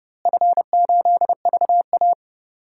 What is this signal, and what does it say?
Morse code, heard as a keyed tone, about 30 wpm, 700 Hz. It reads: F8VA
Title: DXCC-100 Call Signs - 4 Characters - Repeated 2x - 30wpm